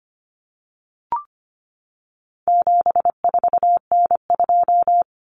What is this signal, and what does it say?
Morse code, heard as a keyed tone, about 25 wpm, 700 Hz. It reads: 74N2
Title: Common Cut Numbers - 4-Digit 25wpm